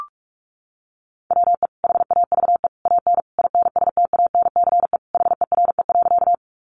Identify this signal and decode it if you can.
Morse code, heard as a keyed tone, about 44 wpm, 700 Hz. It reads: WE HAVE AN INSTANCE HERE.
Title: Sentences from Top 800 Words 45wpm